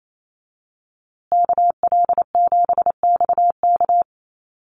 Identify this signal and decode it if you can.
Morse code, heard as a keyed tone, about 28 wpm, 700 Hz. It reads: KL7XK
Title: USA Call Signs Repeated 2x 28wpm